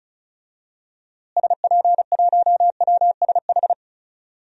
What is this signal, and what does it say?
Morse code, heard as a keyed tone, about 36 wpm, 700 Hz. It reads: SP1WSH
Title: DXCC-100 Call Signs - 6 Characters - Repeated 3x - 35wpm